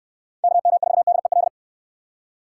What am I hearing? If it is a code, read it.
Morse code, heard as a keyed tone, about 69 wpm, 700 Hz. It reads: KM4DEL